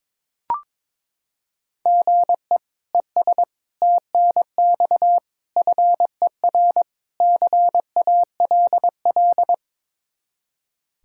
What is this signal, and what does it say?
Morse code, heard as a keyed tone, about 21 wpm, 700 Hz. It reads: GE ES TNX FER CALL
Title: K1USN SST QSO Practice 22wpm